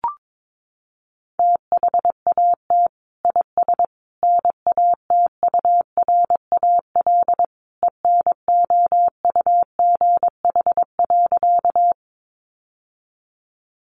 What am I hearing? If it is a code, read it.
Morse code, heard as a keyed tone, about 22 wpm, 700 Hz. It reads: THAT IS NATURAL ENOUGH.
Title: Sentences from Top 700 Words 22wpm